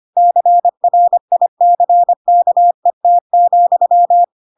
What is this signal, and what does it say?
Morse code, heard as a keyed tone, about 25 wpm, 700 Hz. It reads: CRICKET,
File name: Pinocchio / Pinocchio ch4 25 25 ik1hge